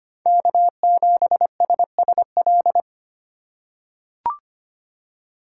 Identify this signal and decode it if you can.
Morse code, heard as a keyed tone, about 25 wpm, 700 Hz. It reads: K7SSL
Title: USA Call Signs - 2x Character Spacing - 25wpm